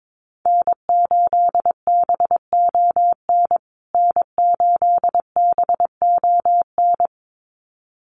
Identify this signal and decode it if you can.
Morse code, heard as a keyed tone, about 22 wpm, 700 Hz. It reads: N8BON N8BON
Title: USA Call Signs Repeated 2x 22wpm